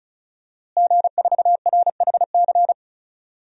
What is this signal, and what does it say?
Morse code, heard as a keyed tone, about 34 wpm, 700 Hz. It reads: G4RHC
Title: DXCC-100 Call Signs - 5 Characters - Repeated 3x - 35wpm